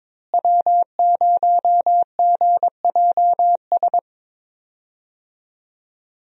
Morse code, heard as a keyed tone, about 22 wpm, 700 Hz. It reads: W0GJS
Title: USA Call Signs - 5 Characters - Repeated 3x - 22wpm